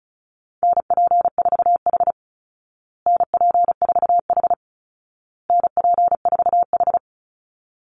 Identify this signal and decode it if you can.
Morse code, heard as a keyed tone, about 34 wpm, 700 Hz. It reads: NP4H NP4H NP4H
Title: Call Signs Repeated 2x - 35wpm